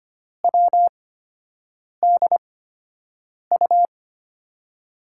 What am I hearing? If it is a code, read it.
Morse code, heard as a keyed tone, about 25 wpm, spaced as for 7 wpm, 700 Hz. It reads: WDU
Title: USA Call Signs - 8x Character Spacing - 25wpm